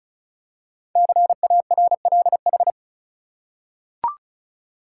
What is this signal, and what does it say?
Morse code, heard as a keyed tone, about 36 wpm, 700 Hz. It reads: CARLH
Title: Code Groups - 5 Letters - 35wpm